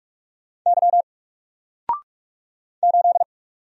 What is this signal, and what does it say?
Morse code, heard as a keyed tone, about 44 wpm, 700 Hz. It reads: Y 8
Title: Single Letter-Number 45wpm